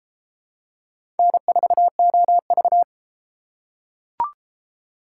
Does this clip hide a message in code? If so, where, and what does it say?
Yes: Morse code from 1.2 to 2.8 s, heard as a keyed tone, about 31 wpm, 700 Hz. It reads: N4OV